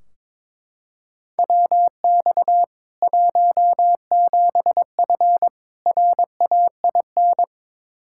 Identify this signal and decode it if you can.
Morse code, heard as a keyed tone, about 22 wpm, 700 Hz. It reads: WX 17F RAIN
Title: K1USN SST QSO Practice 22wpm